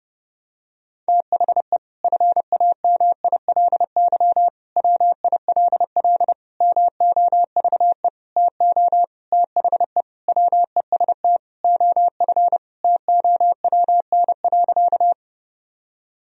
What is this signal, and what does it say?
Morse code, heard as a keyed tone, about 30 wpm, 700 Hz. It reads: THE FAMILY WILL MOVE TO THE WEST OF TOWN.
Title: Sentences from Top 1000 Words - Review Part 2 - 30wpm